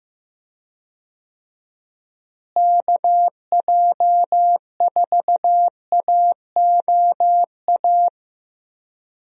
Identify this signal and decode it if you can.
Morse code, heard as a keyed tone, about 15 wpm, 700 Hz. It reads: KJ4AOA
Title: USA Call Signs - 6 Characters - Repeated 2x - 15wpm